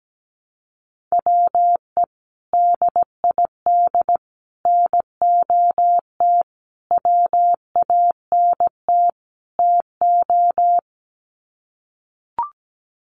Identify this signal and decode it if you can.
Morse code, heard as a keyed tone, about 17 wpm, 700 Hz. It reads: WE DID NOT WANT TO
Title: Sets of 5 Words Encore 17wpm